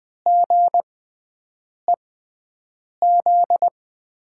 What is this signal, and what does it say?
Morse code, heard as a keyed tone, about 21 wpm, spaced as for 7 wpm, 700 Hz. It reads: GEZ
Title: USA Call Signs - 6 Characters - 6x Character Spacing - 20wpm